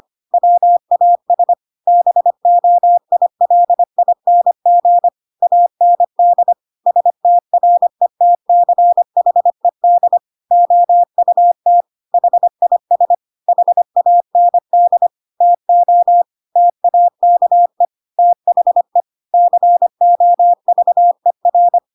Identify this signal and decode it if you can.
Morse code, heard as a keyed tone, about 25 wpm, 700 Hz. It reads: WAS BOILING AND STRETCHED OUT HIS HAND TO TAKE THE COVER
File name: Pinocchio / Pinocchio ch5 25 25 ik1hge